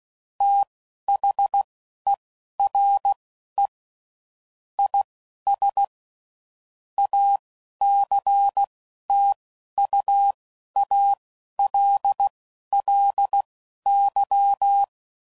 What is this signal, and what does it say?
Morse code, heard as a keyed tone, about 16 wpm, spaced as for 11 wpm, 800 Hz. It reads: THERE IS ACTUALLY
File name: CW Code Groups @10WPM/Technical Subjects